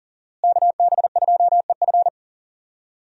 Morse code, heard as a keyed tone, about 40 wpm, 700 Hz. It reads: KB2EF
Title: USA Call Signs - 5 Characters - Repeated 2x - 40wpm